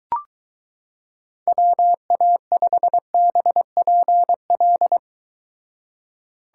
Morse code, heard as a keyed tone, about 23 wpm, 700 Hz. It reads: WA5BPL